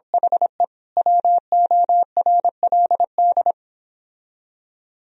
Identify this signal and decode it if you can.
Morse code, heard as a keyed tone, about 26 wpm, 700 Hz. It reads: HE WORLD